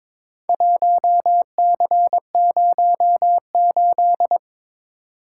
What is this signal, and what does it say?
Morse code, heard as a keyed tone, about 22 wpm, 700 Hz. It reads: 1C08
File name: practice / 22wpm / Four Letter-Number Combinations - 22wpm